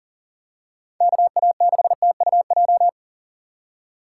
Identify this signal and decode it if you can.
Morse code, heard as a keyed tone, about 40 wpm, 700 Hz. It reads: KA6TUJ
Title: USA Call Signs - 6 Characters - 3x Character Spacing - 40wpm